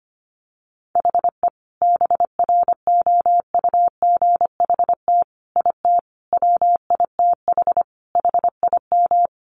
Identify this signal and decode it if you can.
Morse code, heard as a keyed tone, about 25 wpm, 700 Hz. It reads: HE BROUGHT IT WITH HIM
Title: Sentences from Top 500 Words 25wpm